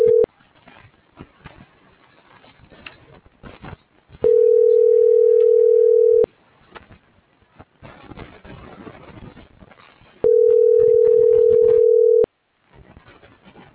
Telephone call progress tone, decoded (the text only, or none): ringback tone